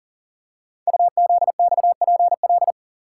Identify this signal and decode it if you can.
Morse code, heard as a keyed tone, about 40 wpm, 700 Hz. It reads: UZXPL
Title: Code Groups - 5 Letters - 40wpm